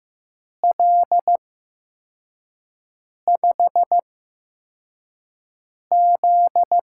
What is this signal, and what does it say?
Morse code, heard as a keyed tone, about 15 wpm, spaced as for 4 wpm, 700 Hz. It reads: L5Z
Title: USA Call Signs - 4 Characters - 8x Character Spacing - 15wpm